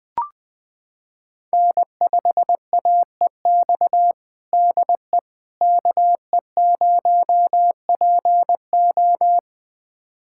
Morse code, heard as a keyed tone, about 20 wpm, 700 Hz. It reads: N5AEX DE KE0PO
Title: USA Callsign DE Callsign - 5 Characters - 20wpm